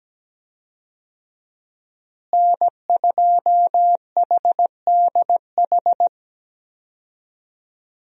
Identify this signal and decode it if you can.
Morse code, heard as a keyed tone, about 17 wpm, 700 Hz. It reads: N2HDH
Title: USA Call Signs - 5 Characters - Repeated 2x - 17wpm